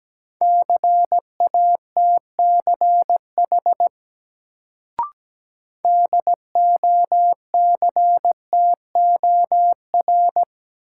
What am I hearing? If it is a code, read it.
Morse code, heard as a keyed tone, about 17 wpm, 700 Hz. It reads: CATCH DOCTOR
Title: Sentences from Top 1500 Words - New Word Drill - 17wpm